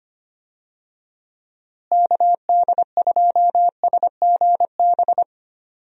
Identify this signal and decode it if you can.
Morse code, heard as a keyed tone, about 25 wpm, 700 Hz. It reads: KD2SGB
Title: USA Call Signs - 6 Characters - Repeated 2x - 25wpm